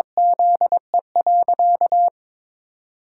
Morse code, heard as a keyed tone, about 22 wpm, 700 Hz. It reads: ZE.
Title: Pangram Challenge v2 - 22wpm